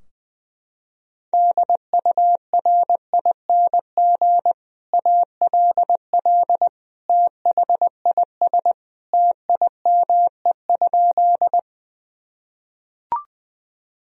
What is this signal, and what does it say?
Morse code, heard as a keyed tone, about 21 wpm, 700 Hz. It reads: DURING ALL THIS TIME?
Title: Sentences from Top 1000 Words - Review Part 2 - 20wpm